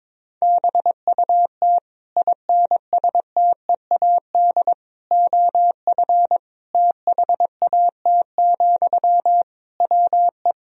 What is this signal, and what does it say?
Morse code, heard as a keyed tone, about 22 wpm, 700 Hz. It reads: BUT INSTEAD OF THAT, WE
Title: Sentences from Top 1000 Words - Review Part 4 - 22wpm